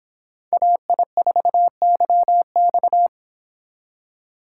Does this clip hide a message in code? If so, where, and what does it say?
Yes: Morse code from 0.5 to 3.1 s, heard as a keyed tone, about 26 wpm, 700 Hz. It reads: AI4YX